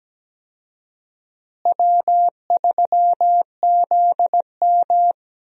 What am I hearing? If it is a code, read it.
Morse code, heard as a keyed tone, about 17 wpm, 700 Hz. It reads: W3ZM
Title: USA Call Signs - 4 Characters - Repeated 3x - 17wpm